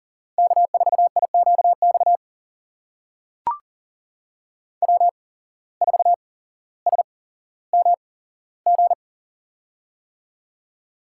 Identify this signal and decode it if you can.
Morse code, heard as a keyed tone, about 39 wpm, spaced as for 17 wpm, 700 Hz. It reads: K4IQX W4SMG
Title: USA Call Signs - 5 Characters - 8x Character Spacing - 40wpm